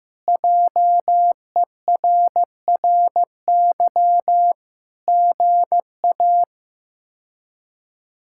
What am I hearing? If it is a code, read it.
Morse code, heard as a keyed tone, about 15 wpm, 700 Hz. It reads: JERRY GA
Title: K1USN SST QSO Practice - 15wpm